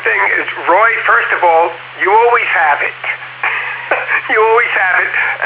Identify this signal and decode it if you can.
Emergency Alert System (SAME) tones, heard as harsh buzzing bursts, with speech. It is heard from 0.9 to 5.1 s.